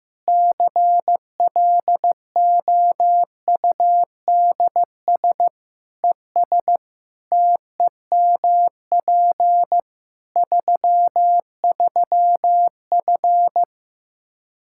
Morse code, heard as a keyed tone, about 16 wpm, 700 Hz. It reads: CLOUDS ES TEMP 33F
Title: K1USN SST QSO Practice 15wpm